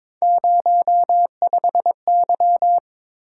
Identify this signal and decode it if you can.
Morse code, heard as a keyed tone, about 23 wpm, 700 Hz. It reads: 05Y
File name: practice / 22wpm / Three Letter-Number Combinations - 22wpm